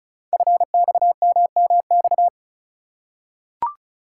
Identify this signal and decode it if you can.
Morse code, heard as a keyed tone, about 36 wpm, 690 Hz. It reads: FXMMX
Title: Code Groups - 5 Letters - 35wpm